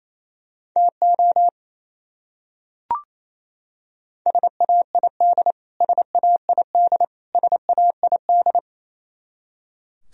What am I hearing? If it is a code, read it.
Morse code, heard as a keyed tone, about 28 wpm, 700 Hz. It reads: TO SAID SAID SAID
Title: Top 100 Words Repeated 28wpm